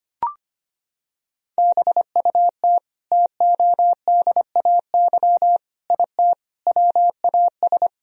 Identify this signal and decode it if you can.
Morse code, heard as a keyed tone, about 25 wpm, 700 Hz. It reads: BUT TODAY IT WAS
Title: Sentences from Top 1000 Words - Review Part 4 - 25wpm